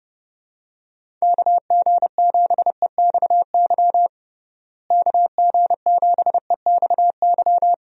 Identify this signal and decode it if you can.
Morse code, heard as a keyed tone, about 30 wpm, 700 Hz. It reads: KG7EXY KG7EXY
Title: USA Call Signs - 6 Characters - Repeated 2x - 30wpm